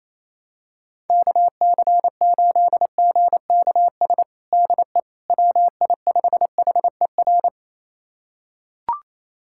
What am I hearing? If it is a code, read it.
Morse code, heard as a keyed tone, about 28 wpm, 700 Hz. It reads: KC8GKS DE WI5HER
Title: USA Callsign DE Callsign 28wpm